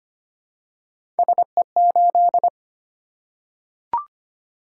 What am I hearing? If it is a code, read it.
Morse code, heard as a keyed tone, about 25 wpm, 700 Hz. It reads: SE8
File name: Three Letter-Number Combinations - 25wpm